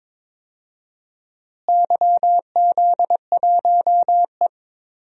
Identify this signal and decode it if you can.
Morse code, heard as a keyed tone, about 22 wpm, 700 Hz. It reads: YZ1E
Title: DXCC-100 Call Signs - 4 Characters - Repeated 2x - 22wpm